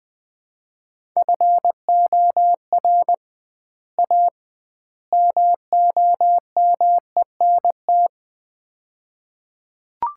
Morse code, heard as a keyed tone, about 20 wpm, 700 Hz. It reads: FOR A MOMENT
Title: 2x Word-Spacing - Sets of 3 Words - 20wpm